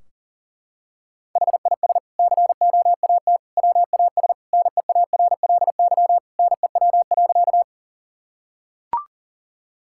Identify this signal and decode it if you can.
Morse code, heard as a keyed tone, about 40 wpm, 700 Hz. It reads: HIS COAT WAS NEARLY NEW.
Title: Sentences from Top 1600 Words 40wpm